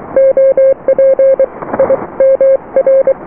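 Morse code, heard as a keyed tone, about 23 wpm, 540 Hz. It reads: OP I MR
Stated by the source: Morse QSO on 14 MHz with a German station, received at RST 579.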